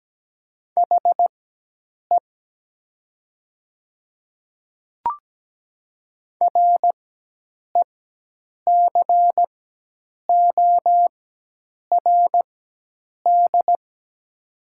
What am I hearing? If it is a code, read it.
Morse code, heard as a keyed tone, about 17 wpm, spaced as for 8 wpm, 700 Hz. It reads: HE RECORD
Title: 4x Character Spacing - Top 500 Words - 17wpm